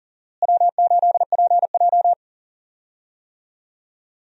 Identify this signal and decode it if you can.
Morse code, heard as a keyed tone, about 39 wpm, 700 Hz. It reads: W8PJ
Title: USA Call Signs - 4 Characters - Repeated 3x - 40wpm